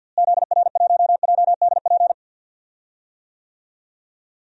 Morse code, heard as a keyed tone, about 52 wpm, 700 Hz. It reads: ZR1JDP
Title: DXCC-100 Call Signs - 6 Characters - Repeated 2x - 50wpm